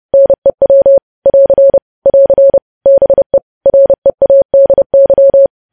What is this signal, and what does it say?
Morse code, heard as a keyed tone, about 30 wpm, 550 Hz. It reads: NEW <AR> <AR> BE READY
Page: news headlines in morse code at 30 wpm